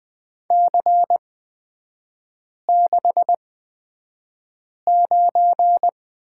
Morse code, heard as a keyed tone, about 20 wpm, spaced as for 5 wpm, 690 Hz. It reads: C69
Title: 229 Lesson 37e - Letter x - Three Characters 20wpm